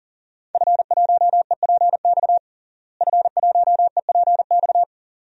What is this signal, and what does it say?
Morse code, heard as a keyed tone, about 40 wpm, 700 Hz. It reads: F1EPX F1EPX
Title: DXCC-100 Call Signs - 5 Characters - Repeated 2x - 40wpm